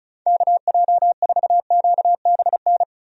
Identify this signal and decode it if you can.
Morse code, heard as a keyed tone, about 34 wpm, 700 Hz. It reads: KJ4QBN
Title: USA Call Signs - 6 Characters - Repeated 2x - 35wpm